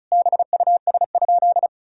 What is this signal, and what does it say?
Morse code, heard as a keyed tone, about 34 wpm, 700 Hz. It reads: BUS?
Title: Sentences from Top 1000 Words - Review Part 6 - 35wpm